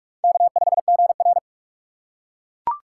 Morse code, heard as a keyed tone, about 41 wpm, 700 Hz. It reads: K5GR
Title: USA Call Signs - 4 Characters - 3x Character Spacing - 45wpm